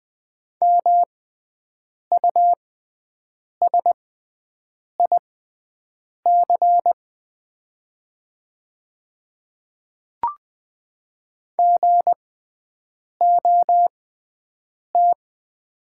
Morse code, heard as a keyed tone, about 20 wpm, spaced as for 7 wpm, 700 Hz. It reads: MUSIC GOT
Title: 6x Character Spacing - Top 300 Words - 20wpm